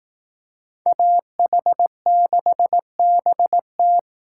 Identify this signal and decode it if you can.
Morse code, heard as a keyed tone, about 18 wpm, 700 Hz. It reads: AH6BT
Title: Speed Racing - USA 2x2 Call Signs 15wpm